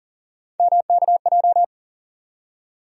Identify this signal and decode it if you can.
Morse code, heard as a keyed tone, about 40 wpm, 700 Hz. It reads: MKJ
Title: USA Call Sign Postfixes - 3 Letters 40wpm